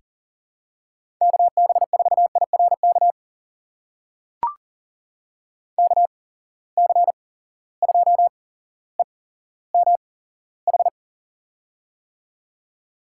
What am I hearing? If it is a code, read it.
Morse code, heard as a keyed tone, about 41 wpm, spaced as for 17 wpm, 700 Hz. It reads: KB4IRK KC2EMH
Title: USA Call Signs - 6 Characters - 8x Character Spacing - 40wpm